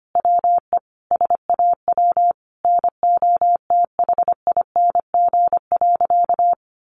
Morse code, heard as a keyed tone, about 25 wpm, 700 Hz. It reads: WE SAW NOTHING.
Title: A Sound Of Thunder 25wpm